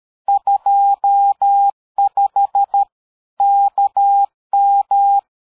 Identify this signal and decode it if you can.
Morse code, heard as a keyed tone, about 13 wpm, 800 Hz. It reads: 25 KM
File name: dk5ke / QSO 60BpM